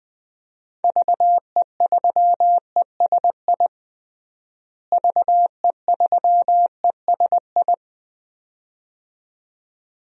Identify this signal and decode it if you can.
Morse code, heard as a keyed tone, about 20 wpm, 700 Hz. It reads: VE3ESI VE3ESI
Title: DXCC-100 Call Signs - 6 Characters - Repeated 2x - 20wpm